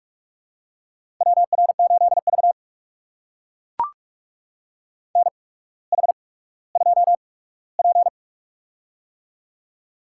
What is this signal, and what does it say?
Morse code, heard as a keyed tone, about 44 wpm, spaced as for 19 wpm, 690 Hz. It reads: WR8V NH2P
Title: USA Call Signs - 4 Characters - 8x Character Spacing - 45wpm